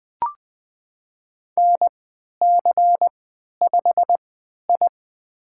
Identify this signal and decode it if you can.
Morse code, heard as a keyed tone, about 20 wpm, spaced as for 11 wpm, 700 Hz. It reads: NC5I
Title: USA Call Signs - 4 Characters - 3x Character Spacing - 20wpm